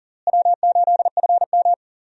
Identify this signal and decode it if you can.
Morse code, heard as a keyed tone, about 40 wpm, 700 Hz. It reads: W8FM
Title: USA Call Signs - 4 Characters - Repeated 2x - 40wpm